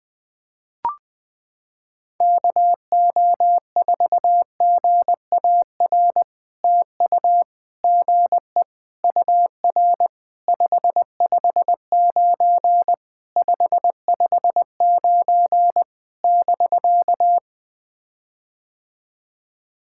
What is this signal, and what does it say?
Morse code, heard as a keyed tone, about 20 wpm, 700 Hz. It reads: KO4GAR TU GE UR 559 559 <BK>
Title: POTA QSO Practice - 20wpm